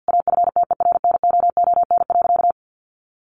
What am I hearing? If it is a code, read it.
Morse code, heard as a keyed tone, about 50 wpm, 700 Hz. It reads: AFTERNOON.
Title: Sentences from Top 1500 Words 50wpm